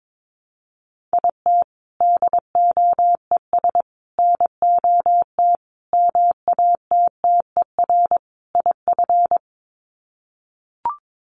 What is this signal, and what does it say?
Morse code, heard as a keyed tone, about 22 wpm, 700 Hz. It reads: IT DOES NOT MATTER IF
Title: Sets of 5 Words Encore 22wpm